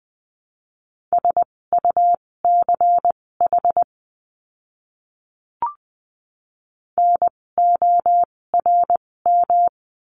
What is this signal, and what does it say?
Morse code, heard as a keyed tone, about 20 wpm, spaced as for 16 wpm, 700 Hz. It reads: SUCH NORM
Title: 256 Lesson 42a - Words 12-20wpm